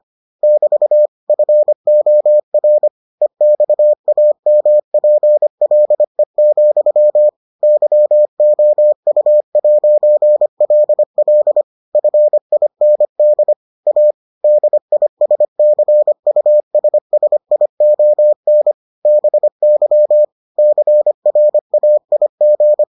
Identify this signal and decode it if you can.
Morse code, heard as a keyed tone, about 25 wpm, 600 Hz. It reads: <BT> FOR EXAMPLE, YOU'LL FIND A DISCUSSION BY CRAIG